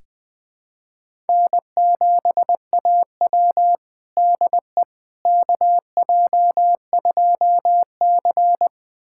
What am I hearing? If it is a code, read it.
Morse code, heard as a keyed tone, about 20 wpm, 700 Hz. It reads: N7AW DE KJ2C
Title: USA Callsign DE Callsign - 4 Characters - 20wpm